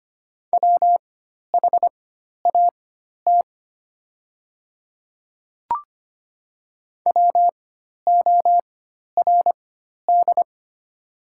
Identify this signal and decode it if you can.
Morse code, heard as a keyed tone, about 25 wpm, spaced as for 12 wpm, 700 Hz. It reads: WHAT WORD
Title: 4x Character Spacing - Top 100 Words - 25wpm